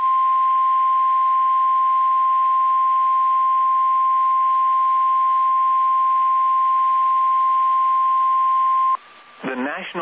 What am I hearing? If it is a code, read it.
Emergency Alert System (SAME) tones (the 1050 Hz attention tone), with speech. It is heard from 0.3 to 9.0 s.
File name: eas / KSDZKDJL / KSDZKDJL